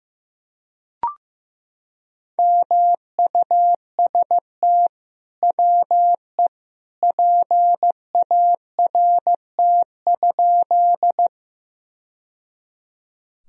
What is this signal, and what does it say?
Morse code, heard as a keyed tone, about 15 wpm, 700 Hz. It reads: MUST WE PART?
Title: Sentences from Top 1000 Words - Review Part 1 - 15wpm